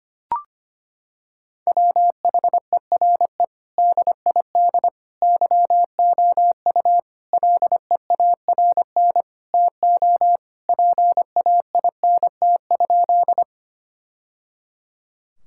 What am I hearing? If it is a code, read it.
Morse code, heard as a keyed tone, about 25 wpm, 700 Hz. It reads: WHERE DID YOU LEARN TO PAINT?
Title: Sentences from Top 1000 Words - Review Part 6 - 25wpm